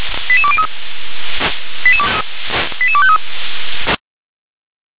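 Five-tone selective call (ZVEI), heard as quick runs of five tones, tones of about 70 ms. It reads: 88203 88253
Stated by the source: Decoded by SDR as ZVEI five-tone selective call.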